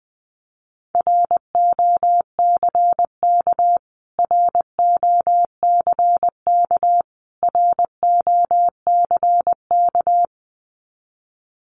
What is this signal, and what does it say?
Morse code, heard as a keyed tone, about 19 wpm, 700 Hz. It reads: ROCK ROCK ROCK
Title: Top 500 Words Repeated 20wpm